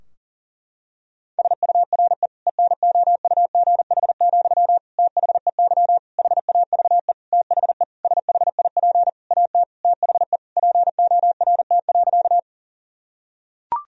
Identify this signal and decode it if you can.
Morse code, heard as a keyed tone, about 37 wpm, 700 Hz. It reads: SURE ENOUGH, THEY HAVE THE SHIP AT THE PORT.
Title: Sentences from Top 1000 Words - Review Part 2 - 40wpm